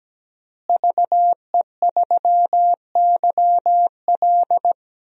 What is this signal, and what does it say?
Morse code, heard as a keyed tone, about 17 wpm, 700 Hz. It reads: VE3YL
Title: DXCC-100 Call Signs - 5 Characters - Repeated 3x - 17wpm